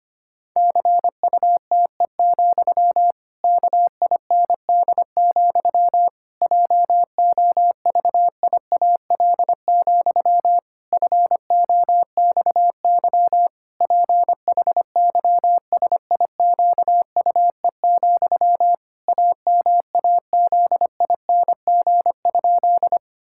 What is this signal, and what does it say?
Morse code, heard as a keyed tone, about 25 wpm, 700 Hz. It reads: CUTE, KIND, JOVIAL, FOXY PHYSIQUE, AMAZING?
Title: Pangram Challenge v2 - 25wpm